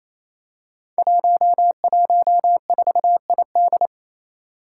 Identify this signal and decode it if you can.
Morse code, heard as a keyed tone, about 28 wpm, 700 Hz. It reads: 114ID
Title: Five Challenging Letter-Number Combinations 28wpm